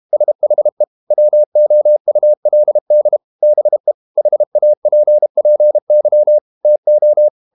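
Morse code, heard as a keyed tone, about 32 wpm, 600 Hz. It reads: SHE WOULD BE HAPPY TO
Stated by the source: Morse code practice at 32 wpm from CW med Gnister, reading Ken Follett's novel Never in English.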